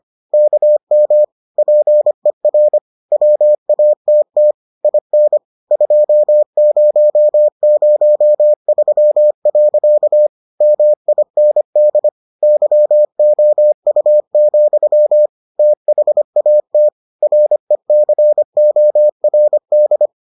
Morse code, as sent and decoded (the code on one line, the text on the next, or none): -.- -- / .--. .-. / .-- .- - - / .. -. / ..--- ----- ----- ...-- .-.-.- / -- .. -. -.. / -.-- --- ..- --..-- / - .... .- - / .-. -.-. --- .-. -..
KM PER WATT IN 2003. MIND YOU, THAT RECORD